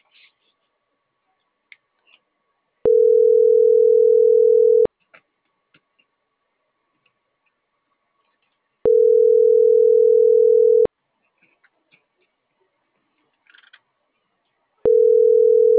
Ringback tone (North America), 440 + 480 Hz.